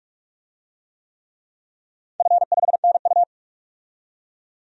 Morse code, heard as a keyed tone, about 44 wpm, 700 Hz. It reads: F5NU
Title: DXCC-100 Call Signs - 4 Characters - 45wpm